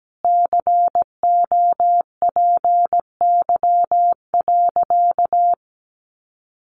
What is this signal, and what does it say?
Morse code, heard as a keyed tone, about 17 wpm, 700 Hz. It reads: COPY.
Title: Sentences from Top 1400 Words 17wpm